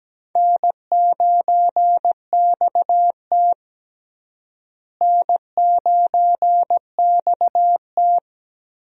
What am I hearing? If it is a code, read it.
Morse code, heard as a keyed tone, about 17 wpm, 700 Hz. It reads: N9XT N9XT
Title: USA Call Signs - 4 Characters - Repeated 3x - 17wpm